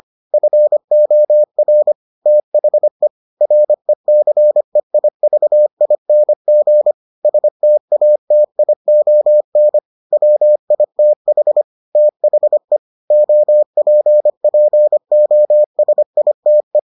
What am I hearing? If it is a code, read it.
Morse code, heard as a keyed tone, about 25 wpm, 600 Hz. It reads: FOR THE RECEIVING STATION WITH THE OPPOSITE